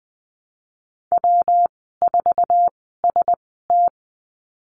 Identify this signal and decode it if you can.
Morse code, heard as a keyed tone, about 20 wpm, spaced as for 15 wpm, 700 Hz. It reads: W4ST
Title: 147 Lesson 24g - Letter y - Call Signs 20wpm